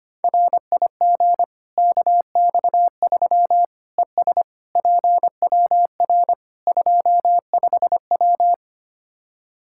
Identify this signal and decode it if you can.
Morse code, heard as a keyed tone, about 25 wpm, 700 Hz. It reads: RIG KX3 ES PWR 25W
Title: Practice QSO - Exchange 2 - Info - 25wpm